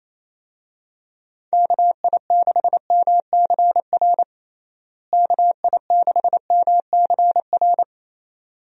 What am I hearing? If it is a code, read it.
Morse code, heard as a keyed tone, about 28 wpm, 700 Hz. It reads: KI6MCR KI6MCR
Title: USA Call Signs - 6 Characters - Repeated 2x - 28wpm